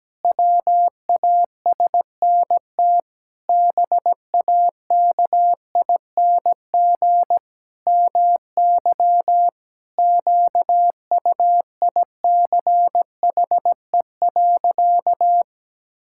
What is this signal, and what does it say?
Morse code, heard as a keyed tone, about 17 wpm, 700 Hz. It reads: WASNT BAKING MY QUICHE.
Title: Pangram Challenge v2 - 17wpm